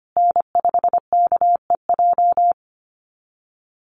Morse code, heard as a keyed tone, about 25 wpm, 700 Hz. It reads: N5KEJ
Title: USA Call Signs Repeated 2x 25wpm